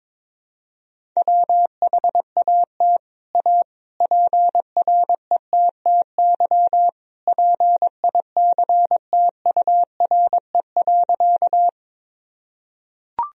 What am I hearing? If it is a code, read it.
Morse code, heard as a keyed tone, about 22 wpm, 700 Hz. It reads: WHAT A PRETTY PICTURE.
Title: Sentences from Top 1000 Words - Review Part 1 - 22wpm